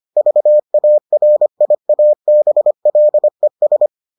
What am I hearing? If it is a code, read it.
Morse code, heard as a keyed tone, about 25 wpm, 600 Hz. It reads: VARIABLES